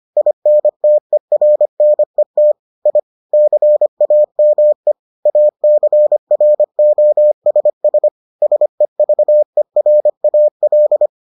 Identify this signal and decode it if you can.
Morse code, heard as a keyed tone, about 25 wpm, 600 Hz. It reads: INTERNET I CAME ACROSS SEVERAL